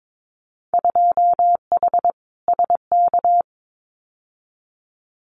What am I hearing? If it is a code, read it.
Morse code, heard as a keyed tone, about 22 wpm, 700 Hz. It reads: 2H SK
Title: Winter Field Day Exchanges Encore 22wpm